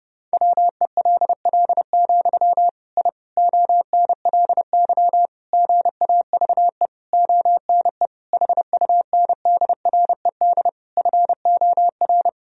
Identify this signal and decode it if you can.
Morse code, heard as a keyed tone, about 30 wpm, 700 Hz. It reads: WELL, I ONLY GAVE ONE HUNDRED FOR
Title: Sentences from Top 1000 Words - Review Part 2 - 30wpm